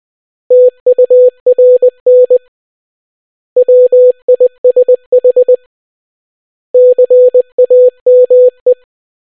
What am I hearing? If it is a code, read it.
Morse code, heard as a keyed tone, about 20 wpm, 500 Hz. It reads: TURN WISH CAME